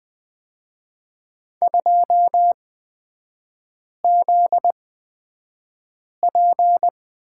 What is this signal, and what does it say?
Morse code, heard as a keyed tone, about 20 wpm, spaced as for 5 wpm, 700 Hz. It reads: 2ZP
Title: 243 Lesson 39e - Letter z - Three Characters 20wpm